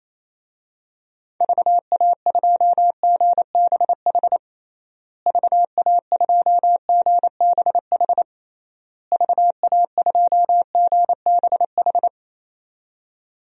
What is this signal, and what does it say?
Morse code, heard as a keyed tone, about 28 wpm, 700 Hz. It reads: VA2GBH VA2GBH VA2GBH
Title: DXCC-100 Call Signs - 6 Characters - Repeated 3x - 28wpm